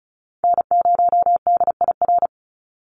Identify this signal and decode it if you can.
Morse code, heard as a keyed tone, about 34 wpm, 700 Hz. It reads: N0DIR
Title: Call Signs Repeated 2x - 35wpm